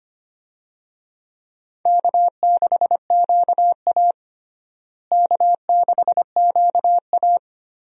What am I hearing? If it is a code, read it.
Morse code, heard as a keyed tone, about 25 wpm, 700 Hz. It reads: K6QA K6QA
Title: USA Call Signs - 4 Characters - Repeated 2x - 25wpm